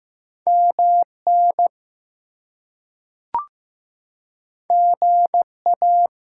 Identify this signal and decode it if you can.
Morse code, heard as a keyed tone, about 15 wpm, 700 Hz. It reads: MN GA
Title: US States and Canadian Provinces 15wpm